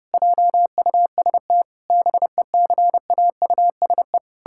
Morse code, heard as a keyed tone, about 32 wpm, 700 Hz. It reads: JUST BECAUSE
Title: Pangram Challenge v2 - 30wpm